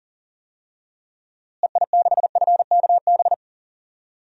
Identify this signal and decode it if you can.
Morse code, heard as a keyed tone, about 39 wpm, 700 Hz. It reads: EI6FKB
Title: DXCC-100 Call Signs - 6 Characters - Repeated 2x - 40wpm